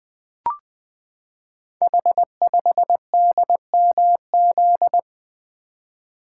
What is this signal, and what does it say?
Morse code, heard as a keyed tone, about 21 wpm, 700 Hz. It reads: H5DMZ
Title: Code Groups - 5 Letter:Numbers - 20wpm